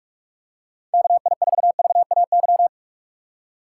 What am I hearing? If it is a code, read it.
Morse code, heard as a keyed tone, about 47 wpm, 700 Hz. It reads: KI4VAY
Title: USA Call Signs - 6x Character Spacing - 45wpm